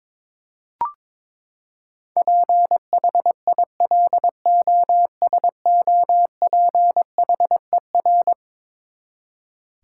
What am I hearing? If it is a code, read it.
Morse code, heard as a keyed tone, about 22 wpm, 700 Hz. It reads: PHILOSOPHER
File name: practice / 22wpm / Eleven Letter Words - 22wpm